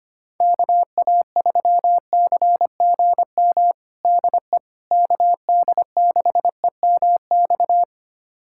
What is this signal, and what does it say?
Morse code, heard as a keyed tone, about 25 wpm, 700 Hz. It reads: KA3CGM DE KD6EMX
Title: USA Callsign DE Callsign 25wpm